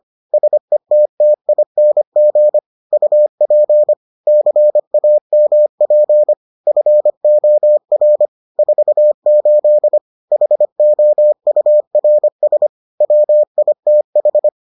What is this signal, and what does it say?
Morse code, heard as a keyed tone, about 25 wpm, 600 Hz. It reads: SETTING UP CAMP FOR 48 HOURS WITH